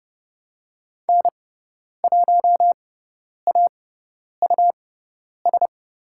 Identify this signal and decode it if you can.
Morse code, heard as a keyed tone, about 30 wpm, spaced as for 10 wpm, 700 Hz. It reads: N1AUS
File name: Lesson 18g - Letter m - Call Signs 30wpm